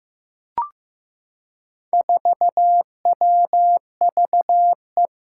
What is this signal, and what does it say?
Morse code, heard as a keyed tone, about 15 wpm, 700 Hz. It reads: 4WVE
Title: Four Challenging Letter-Number Combinations 15wpm